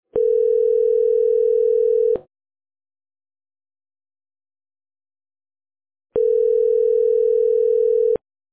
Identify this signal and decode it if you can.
Ringback tone (North America), 440 + 480 Hz.